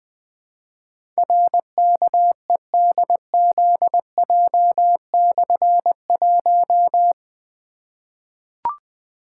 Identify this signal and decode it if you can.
Morse code, heard as a keyed tone, about 21 wpm, 700 Hz. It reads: RKEDZJ/1
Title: Difficult Call Signs 20wpm